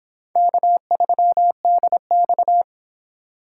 Morse code, heard as a keyed tone, about 26 wpm, 700 Hz. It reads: K3DX